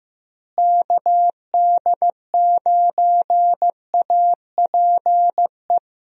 Morse code, heard as a keyed tone, about 15 wpm, 700 Hz. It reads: KD9APE
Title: USA Call Signs - 6 Characters - Repeated 2x - 15wpm